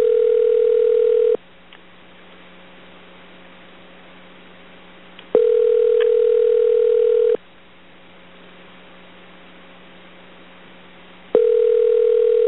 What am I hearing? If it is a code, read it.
Ringback tone (North America), 440 + 480 Hz.